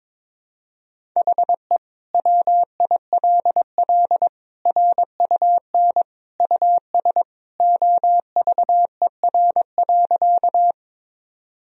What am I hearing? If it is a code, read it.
Morse code, heard as a keyed tone, about 22 wpm, 700 Hz. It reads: HE WILL RUN US OVER.
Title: Sentences from Top 1000 Words - Review Part 1 - 22wpm